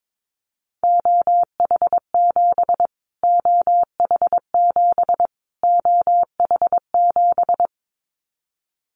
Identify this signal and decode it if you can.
Morse code, heard as a keyed tone, about 22 wpm, 700 Hz. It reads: OH7 OH7 OH7
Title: Callsign Prefixes for Top 25 DXCC - Repeated 3x - 22wpm